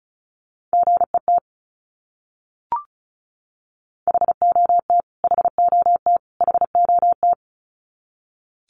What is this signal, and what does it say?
Morse code, heard as a keyed tone, about 34 wpm, 700 Hz. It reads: GET HOT HOT HOT
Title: Top 200 Words Repeated 35wpm